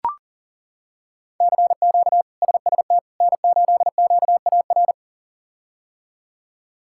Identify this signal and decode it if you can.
Morse code, heard as a keyed tone, about 40 wpm, 700 Hz. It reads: CQ SST N8QAR
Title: K1USN SST QSO Practice - 40wpm